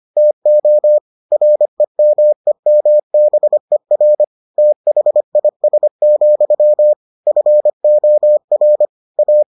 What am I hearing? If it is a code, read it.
Morse code, heard as a keyed tone, about 25 wpm, 600 Hz. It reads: TO REMEMBER THIS, FOR A